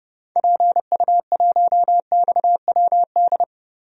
Morse code, heard as a keyed tone, about 29 wpm, 700 Hz. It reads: PU1XWD
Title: DXCC-100 Call Signs - 6 Characters - Repeated 2x - 30wpm